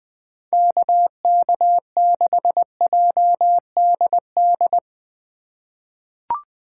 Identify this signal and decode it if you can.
Morse code, heard as a keyed tone, about 21 wpm, 700 Hz. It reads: KK6JDD
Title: USA Call Signs - 3x Character Spacing - 20wpm